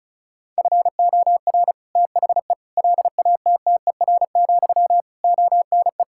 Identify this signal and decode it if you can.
Morse code, heard as a keyed tone, about 36 wpm, 700 Hz. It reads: FOR THE LATTER, ONE
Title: Sentences from Top 1600 Words 35wpm